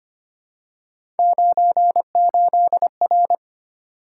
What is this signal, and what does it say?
Morse code, heard as a keyed tone, about 25 wpm, 700 Hz. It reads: 98R
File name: Three Letter-Number Combinations - 25wpm